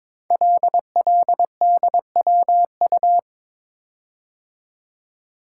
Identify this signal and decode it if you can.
Morse code, heard as a keyed tone, about 22 wpm, 700 Hz. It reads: LLDWU
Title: Code Groups - 5 Letter:Numbers - 22wpm